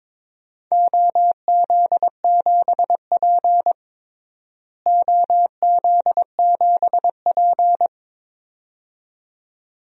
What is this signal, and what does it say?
Morse code, heard as a keyed tone, about 22 wpm, 700 Hz. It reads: OZ7P OZ7P
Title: DXCC-100 Call Signs - 4 Characters - Repeated 2x - 22wpm